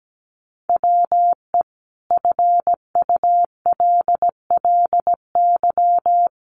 Morse code, heard as a keyed tone, about 17 wpm, 700 Hz. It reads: WE FULLY
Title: Sentences from Top 1300 Words 17wpm